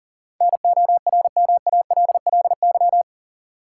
Morse code, heard as a keyed tone, about 40 wpm, 700 Hz. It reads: NORMALLY
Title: Sentences from Top 1000 Words - Review Part 6 - New Word Drill - 40wpm